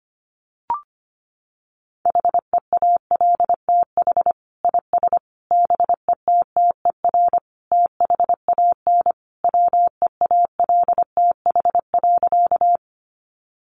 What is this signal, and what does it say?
Morse code, heard as a keyed tone, about 25 wpm, 700 Hz. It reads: HEALTH IS BETTER THAN WEALTH.
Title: Sentences from Top 1200 Words 25wpm